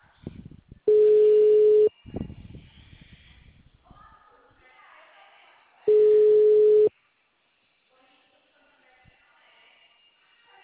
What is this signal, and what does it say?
Ringback tone (Europe, 425 Hz).
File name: out- -2014- - - .9586